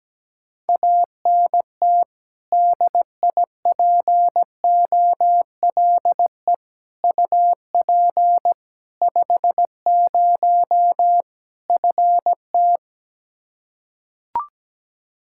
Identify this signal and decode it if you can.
Morse code, heard as a keyed tone, about 17 wpm, 700 Hz. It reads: ANT DIPOLE UP 50 FT
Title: Practice QSO - Exchange 2 - Info - 17wpm